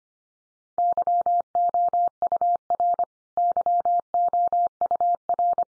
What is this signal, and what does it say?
Morse code, heard as a keyed tone, about 25 wpm, 700 Hz. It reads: YOUR YOUR
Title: Top 100 Words 25wpm x3